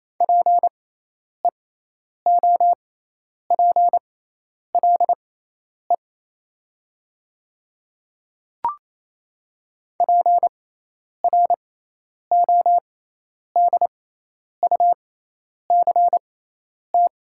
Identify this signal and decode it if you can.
Morse code, heard as a keyed tone, about 28 wpm, spaced as for 10 wpm, 700 Hz. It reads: PEOPLE PRODUCT
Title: 6x Character Spacing - Top 500 Words - 28wpm